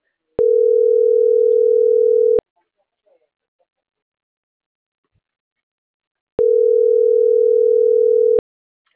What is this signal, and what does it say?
Ringback tone (North America), 440 + 480 Hz.